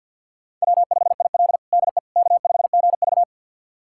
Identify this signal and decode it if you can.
Morse code, heard as a keyed tone, about 50 wpm, 700 Hz. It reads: W5IL DE K5GV
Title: USA Callsign DE Callsign - 4 Characters - 50wpm